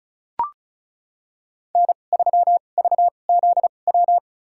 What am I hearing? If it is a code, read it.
Morse code, heard as a keyed tone, about 34 wpm, spaced as for 25 wpm, 700 Hz. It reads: N3VZW
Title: USA Call Signs - 2x Character Spacing - 35wpm